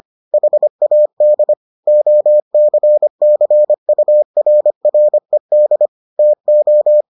Morse code, heard as a keyed tone, about 25 wpm, 600 Hz. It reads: HAD OCCURRED TO